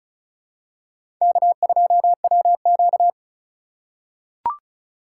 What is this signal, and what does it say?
Morse code, heard as a keyed tone, about 34 wpm, 700 Hz. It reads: K2WQ
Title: USA Call Signs - 4 Characters - 4x Character Spacing - 35wpm